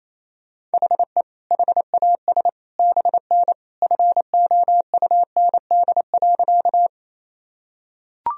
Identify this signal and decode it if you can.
Morse code, heard as a keyed tone, about 28 wpm, 700 Hz. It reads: HE HAS BN FOUND.
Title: Sentences from Top 300 Words and Abbreviations 28wpm